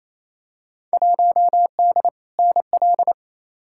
Morse code, heard as a keyed tone, about 28 wpm, 700 Hz. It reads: 1D NL
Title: Field Day Exchanges Encore - 28wpm